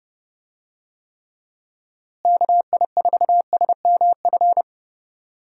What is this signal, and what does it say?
Morse code, heard as a keyed tone, about 30 wpm, 700 Hz. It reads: KI4SMF
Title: USA Call Signs - 6 Characters - Repeated 3x - 30wpm